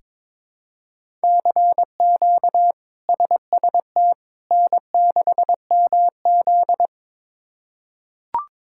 Morse code, heard as a keyed tone, about 22 wpm, 700 Hz. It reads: CQ SST N6MZ
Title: K1USN SST QSO Practice - 22wpm